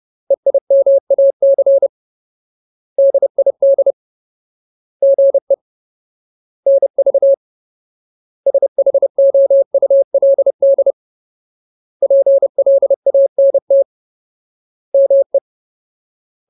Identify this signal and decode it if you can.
Morse code, heard as a keyed tone, about 30 wpm, 550 Hz. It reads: EIMAC DID GE NV SHOULD PLANT ME
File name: Ultimate Word List / 30WPM Ultimate